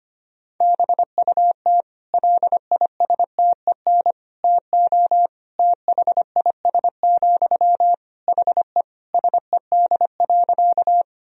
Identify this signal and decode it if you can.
Morse code, heard as a keyed tone, about 25 wpm, 700 Hz. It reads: BUT LISTEN TO THIS, HE SED.
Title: Sentences from Top 400 Words and Abbreviations 25wpm